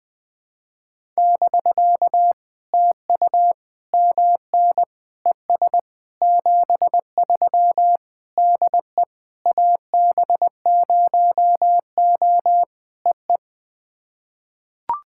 Morse code, heard as a keyed tone, about 20 wpm, 700 Hz. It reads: <BK> TU MN ES 73 DE AB0O EE
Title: POTA QSO Practice - 20wpm